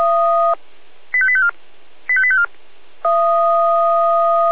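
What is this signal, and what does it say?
Five-tone selective call (ZVEI), heard as quick runs of five tones, tones of about 70 ms. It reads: 85854 85854
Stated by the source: ZVEI five-tone call received by SDR.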